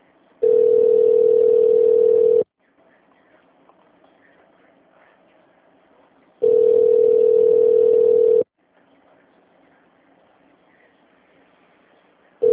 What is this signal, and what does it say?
Ringback tone (North America), 440 + 480 Hz.